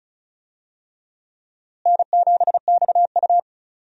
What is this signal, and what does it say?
Morse code, heard as a keyed tone, about 36 wpm, 700 Hz. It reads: N7XU
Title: USA Call Signs - 4 Characters - Repeated 3x - 35wpm